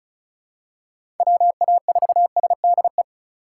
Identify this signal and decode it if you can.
Morse code, heard as a keyed tone, about 36 wpm, 700 Hz. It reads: WA4SDE
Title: USA Call Signs - 8x Character Spacing - 35wpm